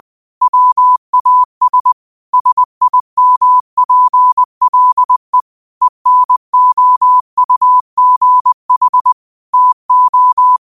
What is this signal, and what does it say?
Morse code, heard as a keyed tone, about 20 wpm, 1000 Hz. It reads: WAS SIMPLE ENOUGH TO
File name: qotdfiles / 20wpm